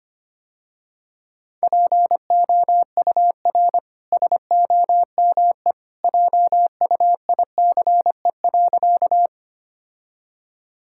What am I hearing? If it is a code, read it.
Morse code, heard as a keyed tone, about 25 wpm, 700 Hz. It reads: POUR SOME JUICE.
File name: Easy Sentences with Hints - 25wpm